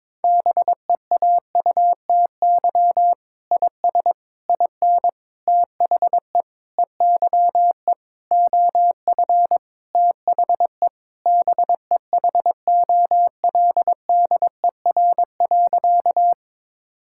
Morse code, heard as a keyed tone, about 22 wpm, 700 Hz. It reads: BEAUTY IS IN THE EYE OF THE BEHOLDER.
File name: Sentences with 7-Word Phrases - 22wpm